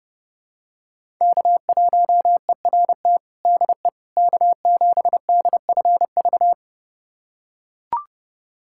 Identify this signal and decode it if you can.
Morse code, heard as a keyed tone, about 29 wpm, 700 Hz. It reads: K1ERT DE K7DFV
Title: USA Callsign DE Callsign - 5 Characters - 30wpm